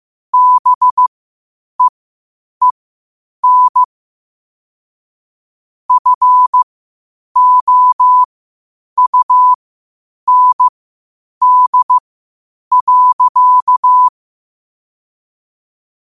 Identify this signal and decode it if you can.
Morse code, heard as a keyed tone, about 15 wpm, spaced as for 8 wpm, 1000 Hz. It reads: BEEN FOUND.
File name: qotdfiles / 8wpm